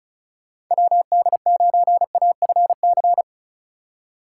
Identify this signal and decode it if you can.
Morse code, heard as a keyed tone, about 43 wpm, 700 Hz. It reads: WD9AFC
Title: Speed Racing - USA 2x3 Call Signs 35wpm